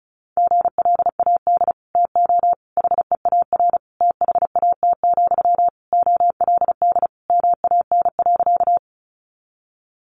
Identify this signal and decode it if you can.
Morse code, heard as a keyed tone, about 34 wpm, 700 Hz. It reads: GLAD TO HEAR THAT, OLD MAN.
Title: Sentences from Top 800 Words 35wpm